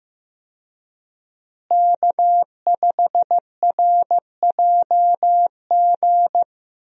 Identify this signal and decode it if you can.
Morse code, heard as a keyed tone, about 15 wpm, 700 Hz. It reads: K5RJG
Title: USA Call Signs - 5 Characters - Repeated 3x - 15wpm